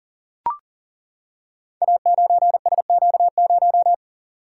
Morse code, heard as a keyed tone, about 39 wpm, 700 Hz. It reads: A9SQ0
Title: Code Groups - 5 Letter:Numbers - 40wpm